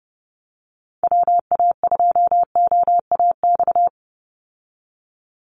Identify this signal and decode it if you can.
Morse code, heard as a keyed tone, about 30 wpm, 700 Hz. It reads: WA2OAX
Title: Call Signs Repeated 2x - 30wpm